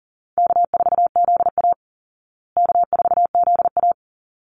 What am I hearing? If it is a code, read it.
Morse code, heard as a keyed tone, about 40 wpm, 700 Hz. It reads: K4ZA K4ZA
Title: Easy Call Signs Repeated 2x - 40wpm